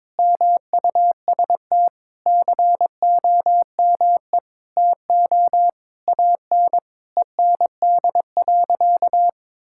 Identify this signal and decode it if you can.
Morse code, heard as a keyed tone, about 22 wpm, 700 Hz. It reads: MUST COME TO AN END.
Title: Sentences from Top 1000 Words - Review Part 5 - 22wpm